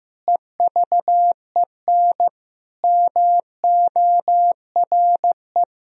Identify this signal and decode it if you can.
Morse code, heard as a keyed tone, about 15 wpm, 700 Hz. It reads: EVEN MORE
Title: Sentences from Top 1000 Words - Review Part 3 - 15wpm